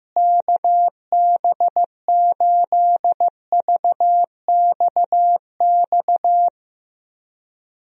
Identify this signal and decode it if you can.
Morse code, heard as a keyed tone, about 15 wpm, 700 Hz. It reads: KB8VXX
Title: USA Call Signs - 6 Characters - Repeated 2x - 15wpm